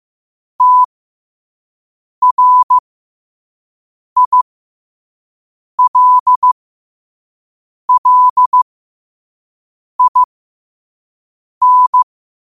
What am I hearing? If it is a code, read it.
Morse code, heard as a keyed tone, about 15 wpm, spaced as for 5 wpm, 1000 Hz. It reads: TRILLIN
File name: qotdfiles / 5wpm